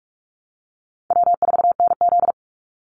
Morse code, heard as a keyed tone, about 46 wpm, 700 Hz. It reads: W4NZ
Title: Easy Call Signs - 45wpm